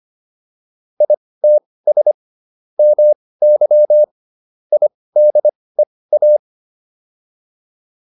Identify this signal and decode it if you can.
Morse code, heard as a keyed tone, about 25 wpm, spaced as for 18 wpm, 600 Hz. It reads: ITS MY IDEA